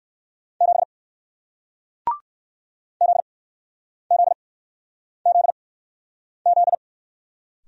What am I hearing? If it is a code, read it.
Morse code, heard as a keyed tone, about 60 wpm, spaced as for 9 wpm, 700 Hz. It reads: 7 ZZZZ